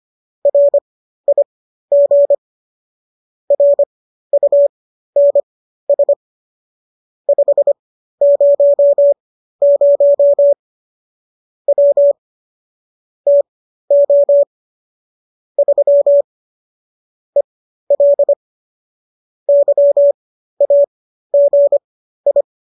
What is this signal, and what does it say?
Morse code, heard as a keyed tone, about 25 wpm, spaced as for 13 wpm, 570 Hz. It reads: RIG RUNS 500 W TO 3 EL YAGI